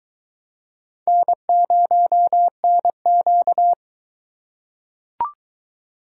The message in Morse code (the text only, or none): N0NQ